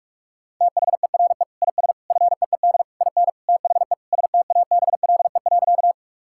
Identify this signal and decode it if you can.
Morse code, heard as a keyed tone, about 46 wpm, 700 Hz. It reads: THERE IS FEED IN THE STABLE.
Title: Sentences from Top 1600 Words 45wpm